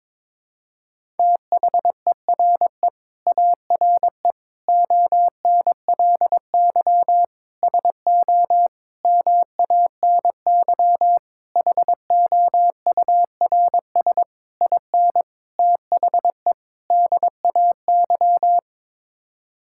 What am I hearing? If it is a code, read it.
Morse code, heard as a keyed tone, about 22 wpm, 700 Hz. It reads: THERE ARE ONLY SO MANY HOURS IN THE DAY
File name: Sets of 9 Words - 22wpm